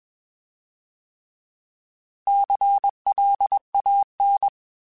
Morse code, heard as a keyed tone, about 21 wpm, 800 Hz. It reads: CLAN